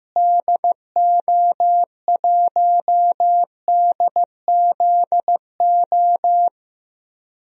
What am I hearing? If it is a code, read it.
Morse code, heard as a keyed tone, about 15 wpm, 700 Hz. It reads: DO1DZO
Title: DXCC-100 Call Signs - 6 Characters - Repeated 2x - 15wpm